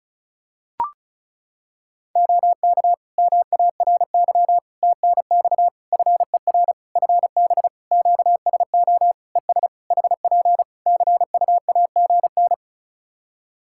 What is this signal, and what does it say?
Morse code, heard as a keyed tone, about 36 wpm, 700 Hz. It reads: OK MARY TNX FER FB QSO ES HP CUAGN
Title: Practice QSO - Exchange 4 - Ending - 35wpm